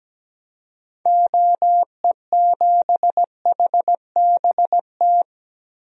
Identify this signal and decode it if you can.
Morse code, heard as a keyed tone, about 17 wpm, 700 Hz. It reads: OE7HBT
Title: DXCC-100 Call Signs - 6 Characters - 17wpm